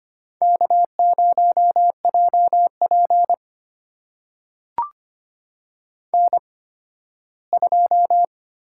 Morse code, heard as a keyed tone, about 25 wpm, spaced as for 15 wpm, 700 Hz. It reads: K0JP N2
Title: USA Call Signs - 8x Character Spacing - 25wpm